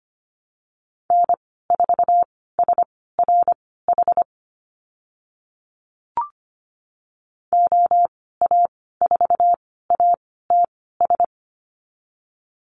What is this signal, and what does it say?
Morse code, heard as a keyed tone, about 25 wpm, spaced as for 16 wpm, 700 Hz. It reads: N4SRH OA4ATS
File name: Lesson 15g - Number 5 - Call Signs 25wpm